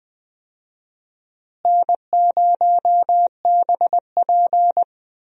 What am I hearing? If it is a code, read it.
Morse code, heard as a keyed tone, about 20 wpm, 700 Hz. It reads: N0BP
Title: USA Call Signs - 4 Characters - Repeated 3x - 20wpm